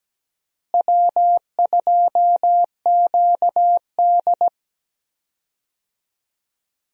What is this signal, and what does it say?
Morse code, heard as a keyed tone, about 17 wpm, 700 Hz. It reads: W2QD
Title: USA Call Signs - 4 Characters - Repeated 3x - 17wpm